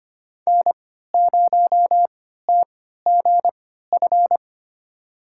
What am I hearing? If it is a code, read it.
Morse code, heard as a keyed tone, about 25 wpm, spaced as for 14 wpm, 700 Hz. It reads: N0TGF
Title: USA Call Signs - 5 Characters - 3x Character Spacing - 25wpm